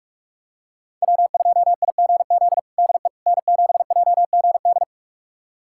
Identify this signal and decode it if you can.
Morse code, heard as a keyed tone, about 45 wpm, 700 Hz. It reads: W2IGZ DE N7JGD
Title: USA Callsign DE Callsign - 5 Characters - 45wpm